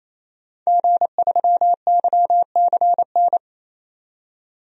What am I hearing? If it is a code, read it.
Morse code, heard as a keyed tone, about 27 wpm, 700 Hz. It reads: G3YCN
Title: DXCC-100 Call Signs - 5 Characters - 28wpm